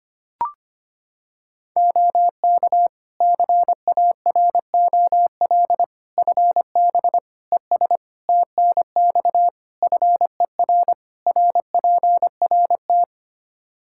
Morse code, heard as a keyed tone, about 25 wpm, 700 Hz. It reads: OK CAROL FB ES TNX FER RPRT
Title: K1USN SST QSO Practice 25wpm